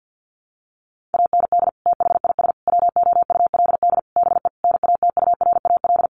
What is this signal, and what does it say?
Morse code, heard as a keyed tone, about 50 wpm, 700 Hz. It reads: AND THIS WOULD BE NATURAL
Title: Sentences from Top 1400 Words 50wpm